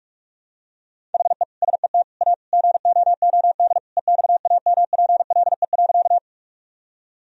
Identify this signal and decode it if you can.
Morse code, heard as a keyed tone, about 41 wpm, 700 Hz. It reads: HE SET A GOOD EXAMPLE.
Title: Sentences from Top 1000 Words - Review Part 1 - 45wpm